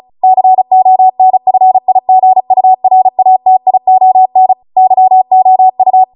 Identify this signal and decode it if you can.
Morse code, heard as a keyed tone, about 36 wpm, 750 Hz. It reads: CONFIGURATION YOU